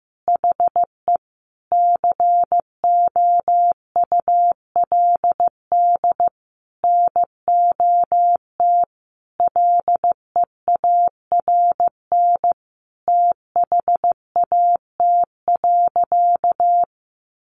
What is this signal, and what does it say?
Morse code, heard as a keyed tone, about 14 wpm, 700 Hz. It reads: HE COULD NOT LEARN THAT.
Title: Sentences from Top 1200 Words 15wpm